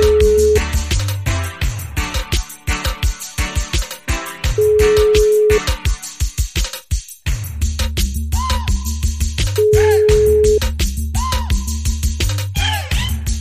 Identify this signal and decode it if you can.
Ringback tone (Europe, 425 Hz).